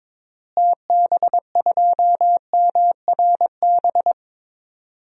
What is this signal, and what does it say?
Morse code, heard as a keyed tone, about 22 wpm, 700 Hz. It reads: TB2MRB
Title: DXCC-100 Call Signs - 6 Characters - Repeated 2x - 22wpm